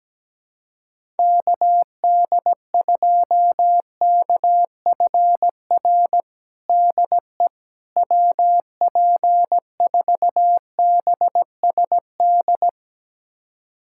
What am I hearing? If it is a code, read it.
Morse code, heard as a keyed tone, about 17 wpm, 700 Hz. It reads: KD2KFR DE WP4BSD
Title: USA Callsign DE Callsign 17wpm